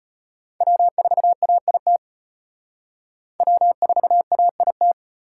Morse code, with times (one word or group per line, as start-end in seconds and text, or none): 0.6-2.0 W4AIT
3.4-4.9 W4AIT